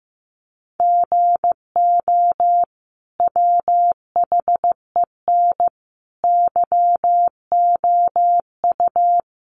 Morse code, heard as a keyed tone, about 15 wpm, 700 Hz. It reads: GO WHEN YOU
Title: Sentences from Top 500 Words 15wpm